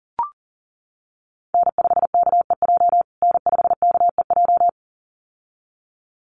Morse code, heard as a keyed tone, about 40 wpm, 700 Hz. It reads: N5KEJ N5KEJ
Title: USA Call Signs Repeated 2x 40wpm